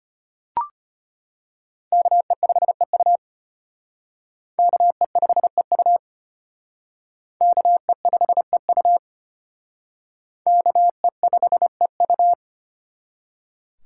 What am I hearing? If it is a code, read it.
Morse code, heard as a keyed tone, about 34 wpm, 700 Hz. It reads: KE5EU KE5EU KE5EU KE5EU